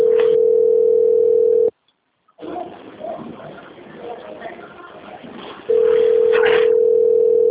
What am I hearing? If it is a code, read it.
Ringback tone (North America), 440 + 480 Hz.